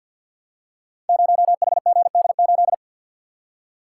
Morse code, heard as a keyed tone, about 48 wpm, 700 Hz. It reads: 0HGD8
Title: Code Groups - 5 Letter:Numbers - 50wpm